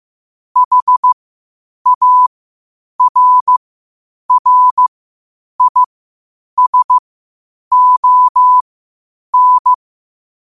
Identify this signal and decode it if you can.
Morse code, heard as a keyed tone, about 15 wpm, spaced as for 8 wpm, 1000 Hz. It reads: HARRISON
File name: qotdfiles / 8wpm